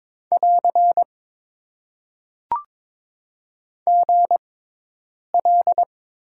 Morse code, heard as a keyed tone, about 21 wpm, spaced as for 8 wpm, 700 Hz. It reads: <AR> GL
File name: Top 90 QSO Elements - 6x Character Spacing - 22wpm